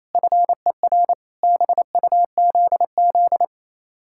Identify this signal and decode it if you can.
Morse code, heard as a keyed tone, about 36 wpm, 700 Hz. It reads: FER BUZZ